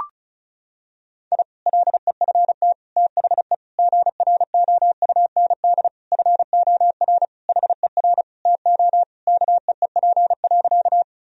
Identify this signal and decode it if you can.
Morse code, heard as a keyed tone, about 36 wpm, 700 Hz. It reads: I LEFT THE GROUND FOR HER TO KEEP.
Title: Sentences from Top 1000 Words - Review Part 2 - 35wpm